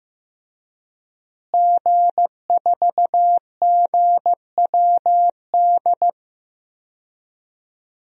Morse code, heard as a keyed tone, about 16 wpm, 700 Hz. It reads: G4GWD
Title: DXCC-100 Call Signs - 5 Characters - Repeated 2x - 15wpm